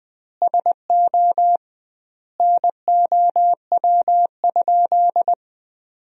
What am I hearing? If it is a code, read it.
Morse code, heard as a keyed tone, about 21 wpm, 700 Hz. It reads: SO NOW?
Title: 2x Word-Spacing - Sentences from Top 200 Words - 20wpm